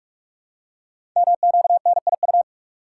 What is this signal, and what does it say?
Morse code, heard as a keyed tone, about 46 wpm, 700 Hz. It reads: MQNIU
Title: Code Groups - 5 Letters - 45wpm